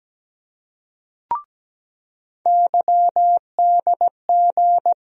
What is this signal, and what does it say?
Morse code, heard as a keyed tone, about 17 wpm, 700 Hz. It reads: YDG
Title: USA Call Sign Postfixes - 3 Letters 17wpm